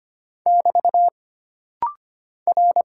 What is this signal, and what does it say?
Morse code, heard as a keyed tone, about 25 wpm, 700 Hz. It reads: <BT> R
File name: Single Character (Letter:Number:Punctuation:Prosign) - Rapid-Fire - 25wpm